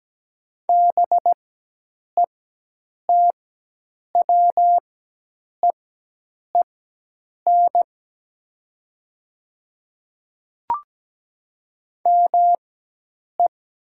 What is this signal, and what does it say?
Morse code, heard as a keyed tone, about 17 wpm, spaced as for 8 wpm, 700 Hz. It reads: BETWEEN ME
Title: 4x Character Spacing - Top 300 Words - 17wpm